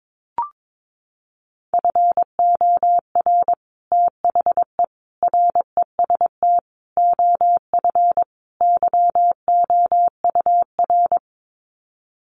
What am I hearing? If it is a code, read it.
Morse code, heard as a keyed tone, about 22 wpm, 700 Hz. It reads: FOR THE REST OF YOUR
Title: Sets of 5 Words Encore 22wpm